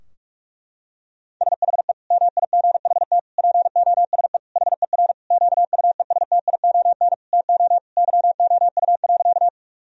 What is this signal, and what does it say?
Morse code, heard as a keyed tone, about 41 wpm, 700 Hz. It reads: SHE MIGHT POSE HER QUESTION TO YOU.
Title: Sentences from Top 1000 Words - Review Part 2 - 45wpm